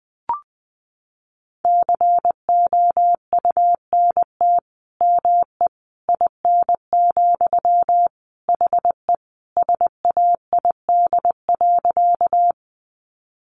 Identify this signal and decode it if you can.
Morse code, heard as a keyed tone, about 20 wpm, 700 Hz. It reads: COUNT ME IN, HE SAID.
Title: Sentences from Top 1500 Words 20wpm